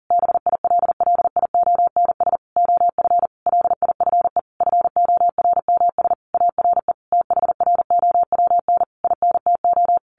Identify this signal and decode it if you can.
Morse code, heard as a keyed tone, about 40 wpm, 700 Hz. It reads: BILLIONS OF LIFE FORMS ARE THROWN INTO
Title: A Sound Of Thunder 40wpm